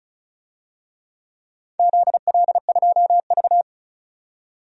Morse code, heard as a keyed tone, about 36 wpm, 700 Hz. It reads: ZL2V
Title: DXCC-100 Call Signs - 4 Characters - 35wpm